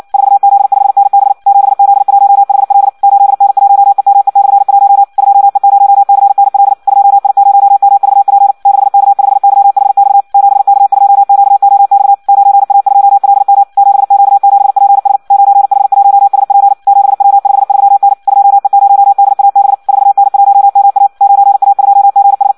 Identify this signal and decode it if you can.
Morse code, heard as a keyed tone, about 56 wpm, 800 Hz. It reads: 4Z5NB 7Z1HL 8N1EME/1 2E0CNL 2I0MVP 6K5YHX 7K2QOX 9A2LM 7Q7PS 9H1SP 7W53N 2E0DAB 4N1MA 9A2GS
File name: Audio practica CW / qrq-tx-58 2